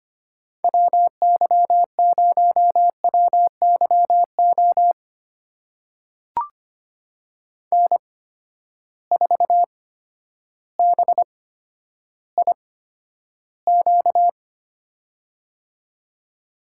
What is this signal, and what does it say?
Morse code, heard as a keyed tone, about 25 wpm, spaced as for 11 wpm, 700 Hz. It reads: WY0WYO N4BIQ
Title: USA Call Signs - 8x Character Spacing - 25wpm